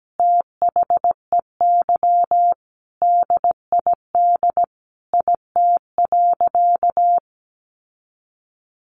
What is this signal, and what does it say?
Morse code, heard as a keyed tone, about 17 wpm, 700 Hz. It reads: THEY DID IT.
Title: Sentences from Top 1200 Words 17wpm